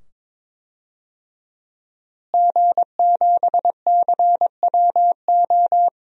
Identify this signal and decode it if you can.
Morse code, heard as a keyed tone, about 22 wpm, 700 Hz. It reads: G7CWO
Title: DXCC-100 Call Signs - 5 Characters - Repeated 2x - 22wpm